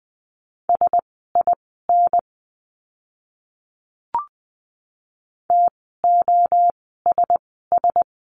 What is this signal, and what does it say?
Morse code, heard as a keyed tone, about 20 wpm, spaced as for 14 wpm, 700 Hz. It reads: SIN TOSS
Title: 036 Lesson 7f - Letter s - Words 20wpm